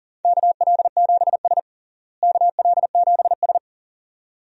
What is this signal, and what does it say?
Morse code, heard as a keyed tone, about 40 wpm, 700 Hz. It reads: KL7S KL7S
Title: DXCC-100 Call Signs - 4 Characters - Repeated 2x - 40wpm